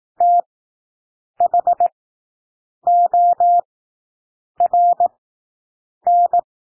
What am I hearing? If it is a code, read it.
Morse code, heard as a keyed tone, about 18 wpm, spaced as for 7 wpm, 700 Hz. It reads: THORN
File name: SOUND WORDS 18 wpm